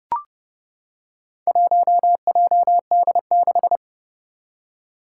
Morse code, heard as a keyed tone, about 30 wpm, 690 Hz. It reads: 1JD6
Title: Four Challenging Letter-Number Combinations 30wpm